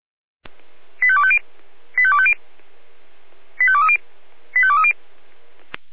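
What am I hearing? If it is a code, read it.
Five-tone selective call (ZVEI), heard as quick runs of five tones, tones of about 70 ms. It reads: 85250 85250 85230 85230
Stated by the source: ZVEI five-tone selective call received with an SDR.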